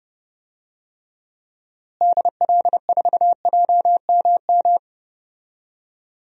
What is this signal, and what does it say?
Morse code, heard as a keyed tone, about 32 wpm, 700 Hz. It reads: DL4JMM
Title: DXCC-100 Call Signs - 6 Characters - Repeated 2x - 30wpm